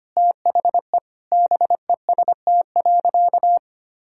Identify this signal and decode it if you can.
Morse code, heard as a keyed tone, about 25 wpm, 700 Hz. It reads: THE BEST.
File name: practice / 25wpm / Sentences with 4-Word Phrases - 25wpm